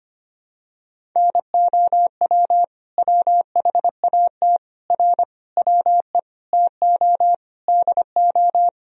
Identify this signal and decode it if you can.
Morse code, heard as a keyed tone, about 25 wpm, 700 Hz. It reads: NOW WHAT R WE TO DO
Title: Sentences from Top 100 Words and Abbreviations 25wpm